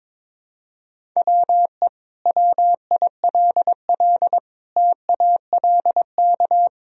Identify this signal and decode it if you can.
Morse code, heard as a keyed tone, about 22 wpm, 700 Hz. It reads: WE WILL TALK
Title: Sentences from Top 400 Words and Abbreviations 22wpm